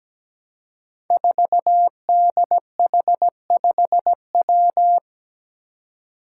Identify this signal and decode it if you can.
Morse code, heard as a keyed tone, about 17 wpm, 700 Hz. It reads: 4DH5W
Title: Five Challenging Letter-Number Combinations 17wpm